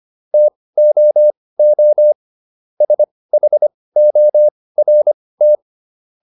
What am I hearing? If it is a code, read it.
Morse code, heard as a keyed tone, about 25 wpm, spaced as for 18 wpm, 600 Hz. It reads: TOO SHORT